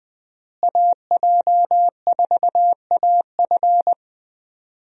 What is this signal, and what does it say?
Morse code, heard as a keyed tone, about 20 wpm, 700 Hz. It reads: AJ4AF
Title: USA Call Signs - 5 Characters - Repeated 3x - 20wpm